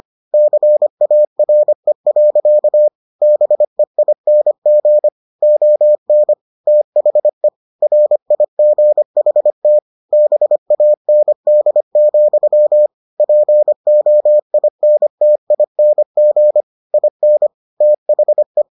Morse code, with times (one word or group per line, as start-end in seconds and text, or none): 0.3-2.9 CARE.
3.2-5.1 BEING
5.4-6.3 ON
6.7-7.5 THE
7.8-9.8 RIGHT
10.1-12.9 BAND,
13.2-16.6 POINTING
16.9-17.5 IN
17.8-18.6 THE